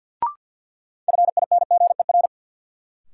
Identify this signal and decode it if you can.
Morse code, heard as a keyed tone, about 53 wpm, 700 Hz. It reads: FINGER
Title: Sentences from Top 1600 Words - New Word Drill - 50wpm